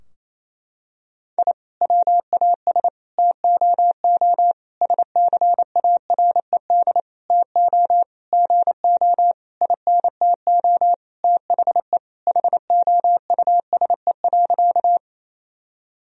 Morse code, heard as a keyed tone, about 28 wpm, 700 Hz. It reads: I WAS TOO SCARED TO GO INTO THE HOUSE.
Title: Sentences from Top 1000 Words - Review Part 4 - 28wpm